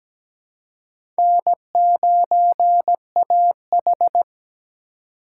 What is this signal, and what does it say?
Morse code, heard as a keyed tone, about 17 wpm, 700 Hz. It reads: N9AH
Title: USA Call Signs - 4 Characters - 2x Character Spacing - 17wpm